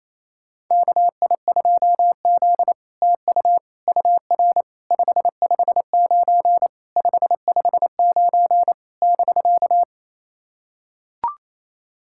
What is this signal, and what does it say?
Morse code, heard as a keyed tone, about 29 wpm, 690 Hz. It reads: KI2Z TU UR 559 559 <BK>
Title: SOTA QSO Practice 28wpm